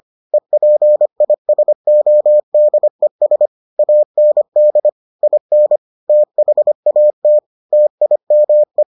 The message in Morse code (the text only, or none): EPISODES AND IN THAT TIME